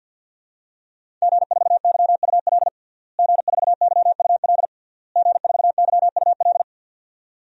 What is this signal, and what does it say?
Morse code, heard as a keyed tone, about 49 wpm, 700 Hz. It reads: G4YUL G4YUL G4YUL
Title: DXCC-100 Call Signs - 5 Characters - Repeated 3x - 50wpm